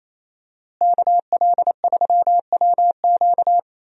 Morse code, heard as a keyed tone, about 28 wpm, 700 Hz. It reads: KL3WQ
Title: USA Call Signs - 5 Characters - Repeated 3x - 28wpm